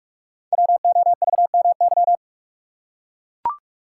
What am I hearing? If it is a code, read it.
Morse code, heard as a keyed tone, about 44 wpm, 690 Hz. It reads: WOVMY
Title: Code Groups - 5 Letters - 45wpm